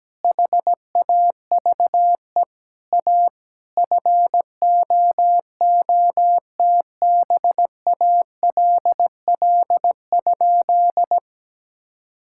Morse code, heard as a keyed tone, about 17 wpm, 700 Hz. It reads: HAVE A FOOTBALL?
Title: Sentences from Top 1600 Words 17wpm